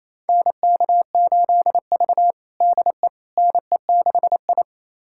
Morse code, heard as a keyed tone, about 28 wpm, 700 Hz. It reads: NK8V DE NE6I
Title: USA Callsign DE Callsign - 4 Characters - 28wpm